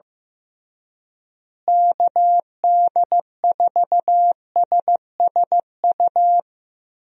Morse code, heard as a keyed tone, about 15 wpm, 700 Hz. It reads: KD4SSU
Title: USA Call Signs - 6 Characters - Repeated 3x - 15wpm